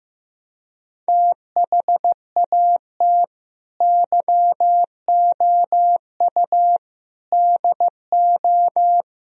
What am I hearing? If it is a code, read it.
Morse code, heard as a keyed tone, about 16 wpm, 700 Hz. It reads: THAT YOU DO
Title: Speed-Racing - Sets of 4 Words 15wpm